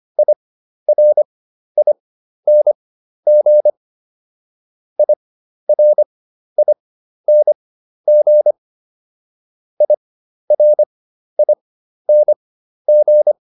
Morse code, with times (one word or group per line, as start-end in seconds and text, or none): 0.2-3.7 IRING
5.0-8.5 IRING
9.8-13.3 IRING